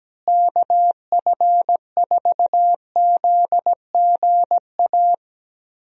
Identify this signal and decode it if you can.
Morse code, heard as a keyed tone, about 17 wpm, 700 Hz. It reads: KF4ZGA
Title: USA Call Signs - 6 Characters - Repeated 3x - 17wpm